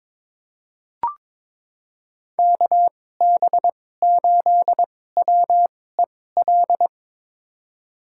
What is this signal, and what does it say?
Morse code, heard as a keyed tone, about 22 wpm, spaced as for 16 wpm, 700 Hz. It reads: KB8WEL
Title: USA Call Signs - 6 Characters - 2x Character Spacing - 22wpm